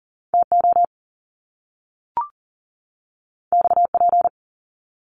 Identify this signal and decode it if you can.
Morse code, heard as a keyed tone, about 41 wpm, 700 Hz. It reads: TO XP
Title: Top Two Letter Combinations in Words 40wpm